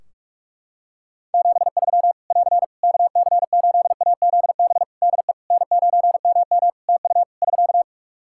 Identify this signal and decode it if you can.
Morse code, heard as a keyed tone, about 46 wpm, 700 Hz. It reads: 73 <AR> KC8AZB DE N9MM TU <SK>
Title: K1USN SST QSO Practice 45wpm